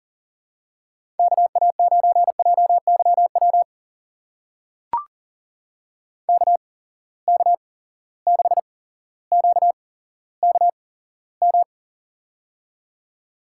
Morse code, heard as a keyed tone, about 40 wpm, spaced as for 17 wpm, 700 Hz. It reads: KA9JYW KK6QKM
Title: USA Call Signs - 8x Character Spacing - 40wpm